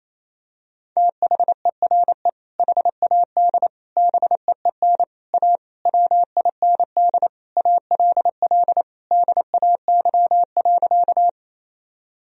Morse code, heard as a keyed tone, about 28 wpm, 700 Hz. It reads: THERE HAD BEEN A WIND ALL DAY.
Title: Sentences from Top 1000 Words - Review Part 2 - 28wpm